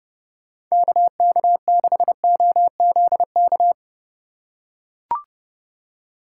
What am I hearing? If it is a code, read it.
Morse code, heard as a keyed tone, about 29 wpm, 700 Hz. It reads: KK6OZK